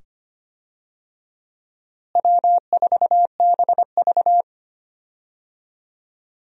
Morse code, heard as a keyed tone, about 25 wpm, 700 Hz. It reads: W4BV
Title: USA Call Signs - 4 Characters - Repeated 2x - 25wpm